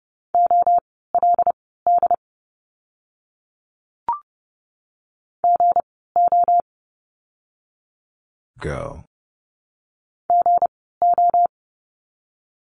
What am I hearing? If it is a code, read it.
Morse code, heard as a keyed tone, about 30 wpm, spaced as for 17 wpm, 700 Hz. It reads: OLD GO GO
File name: course / 30:10wpm / 237 Lesson 38f - Letter q - Words 30wpm